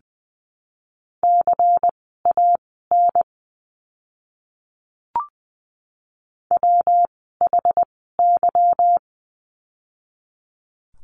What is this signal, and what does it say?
Morse code, heard as a keyed tone, about 20 wpm, spaced as for 15 wpm, 700 Hz. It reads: CAN WHY
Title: 146 Lesson 24f - Letter y - Words 20wpm